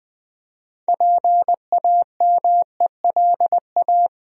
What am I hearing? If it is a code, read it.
Morse code, heard as a keyed tone, about 20 wpm, 700 Hz. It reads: PAMELA
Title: USA Common Names - 20wpm